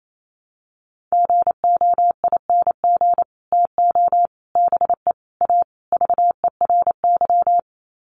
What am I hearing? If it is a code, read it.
Morse code, heard as a keyed tone, about 28 wpm, 700 Hz. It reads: GOING TO BE A VERY
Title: Sets of 5 Words Encore 28wpm